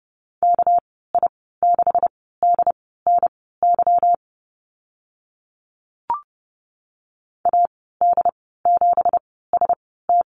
Morse code, heard as a keyed tone, about 30 wpm, spaced as for 17 wpm, 700 Hz. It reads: KI6DNY AD7ST
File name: Lesson 37g - Letter x - Call Signs 30wpm